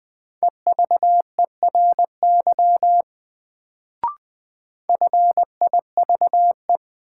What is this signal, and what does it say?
Morse code, heard as a keyed tone, about 20 wpm, 700 Hz. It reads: EVERY FIVE
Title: Sets of 5 Words - New Word Drill - 20wpm